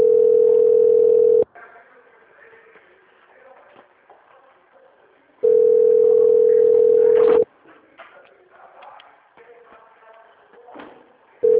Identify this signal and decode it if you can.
Ringback tone (North America), 440 + 480 Hz.
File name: out- - - - - .9503